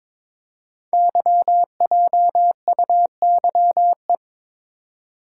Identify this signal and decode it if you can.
Morse code, heard as a keyed tone, about 22 wpm, 700 Hz. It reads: YJUYE
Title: Code Groups - 5 Letters - 22wpm